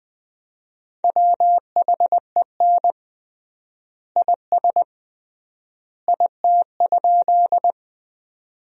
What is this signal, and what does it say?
Morse code, heard as a keyed tone, about 20 wpm, 700 Hz. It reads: WHEN IS IT?
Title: 3x Word-Spacing - Sentences from Top 100 Words - 20wpm